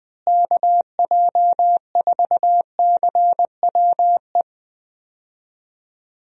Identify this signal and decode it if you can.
Morse code, heard as a keyed tone, about 21 wpm, 700 Hz. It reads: KJ4CWE
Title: USA Call Signs - 6 Characters - Repeated 2x - 20wpm